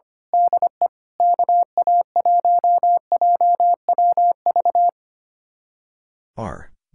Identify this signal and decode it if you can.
Morse code, heard as a keyed tone, about 26 wpm, 700 Hz. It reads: DE KA1JWV
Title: Speed-Racing Common QSO Phrases - 25wpm